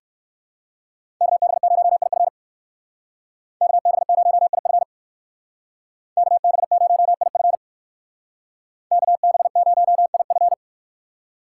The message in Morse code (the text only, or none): KB0IF KB0IF KB0IF KB0IF